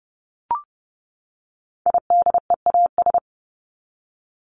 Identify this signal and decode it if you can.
Morse code, heard as a keyed tone, about 30 wpm, 700 Hz. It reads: IDEAS
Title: ES:S Suffix - Top 100 Words with Base Words 30wpm